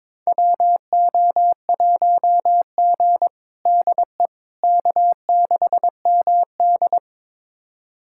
Morse code, heard as a keyed tone, about 22 wpm, 700 Hz. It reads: WO1G DE K6MD
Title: USA Callsign DE Callsign - 4 Characters - 22wpm